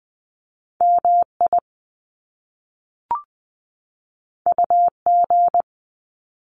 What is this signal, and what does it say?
Morse code, heard as a keyed tone, about 20 wpm, 700 Hz. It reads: M I U G
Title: Top Two Letter Combinations in Words 20wpm